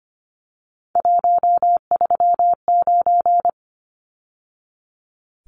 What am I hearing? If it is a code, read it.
Morse code, heard as a keyed tone, about 25 wpm, 690 Hz. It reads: 139
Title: Numbers - 3-Digit 25wpm